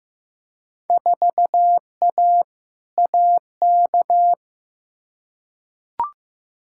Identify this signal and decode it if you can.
Morse code, heard as a keyed tone, about 15 wpm, 700 Hz. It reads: 4A AK
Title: Field Day Exchanges Encore - 15wpm